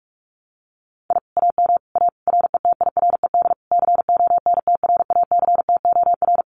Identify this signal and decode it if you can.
Morse code, heard as a keyed tone, about 46 wpm, 700 Hz. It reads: I AM A RETIRED CONTRACTOR
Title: Common QSO Phrases Encore 45wpm